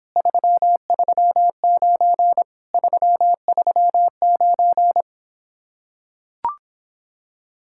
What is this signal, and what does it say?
Morse code, heard as a keyed tone, about 26 wpm, 700 Hz. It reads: 339 339